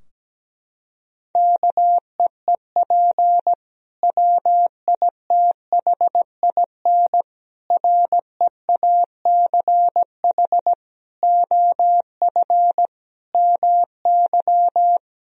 Morse code, heard as a keyed tone, about 17 wpm, 700 Hz. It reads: KEEP WITHIN REACH OF MY
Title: Sentences from Top 1000 Words - Review Part 4 - 17wpm